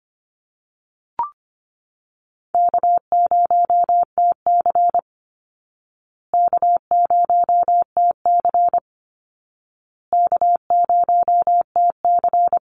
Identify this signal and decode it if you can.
Morse code, heard as a keyed tone, about 25 wpm, 700 Hz. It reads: K0TC K0TC K0TC
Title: Easy Call Signs Repeated 3x - 25wpm